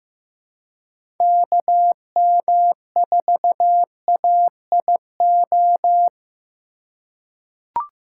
Morse code, heard as a keyed tone, about 15 wpm, 700 Hz. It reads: KM4AIO
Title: USA Call Signs - 4x Character Spacing - 15wpm